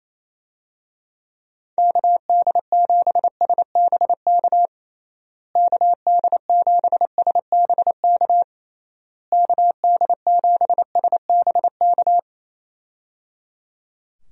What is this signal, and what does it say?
Morse code, heard as a keyed tone, about 28 wpm, 700 Hz. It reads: KD7SBK KD7SBK KD7SBK
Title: USA Call Signs - 6 Characters - Repeated 3x - 28wpm